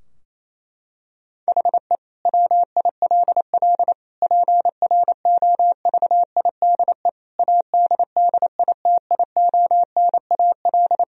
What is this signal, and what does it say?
Morse code, heard as a keyed tone, about 28 wpm, 700 Hz. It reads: HE WILL PROVIDE ADDITIONAL
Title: Sentences from Top 1000 Words - Review Part 4 - 28wpm